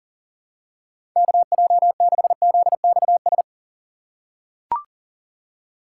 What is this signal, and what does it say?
Morse code, heard as a keyed tone, about 41 wpm, 700 Hz. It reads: KJ6ZXS
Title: USA Call Signs - 6 Characters - 8x Character Spacing - 40wpm